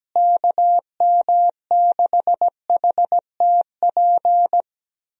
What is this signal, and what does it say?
Morse code, heard as a keyed tone, about 17 wpm, 700 Hz. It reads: KM6HTP
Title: USA Call Signs - 6 Characters - Repeated 2x - 17wpm